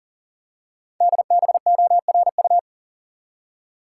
Morse code, heard as a keyed tone, about 40 wpm, 700 Hz. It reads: DBORU
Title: Code Groups - 5 Letters - 40wpm